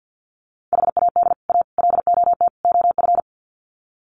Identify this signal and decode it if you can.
Morse code, heard as a keyed tone, about 49 wpm, 700 Hz. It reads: HAD A LOT OF
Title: Sets of 4 Words 50wpm